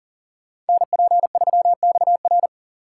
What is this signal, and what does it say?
Morse code, heard as a keyed tone, about 49 wpm, 700 Hz. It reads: NP3XR
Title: Speed Racing - USA Call Signs 40wpm